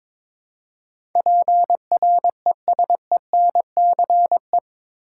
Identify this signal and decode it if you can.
Morse code, heard as a keyed tone, about 22 wpm, 700 Hz. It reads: PRESENCE
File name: practice / 22wpm / Eight Letter Words - 22wpm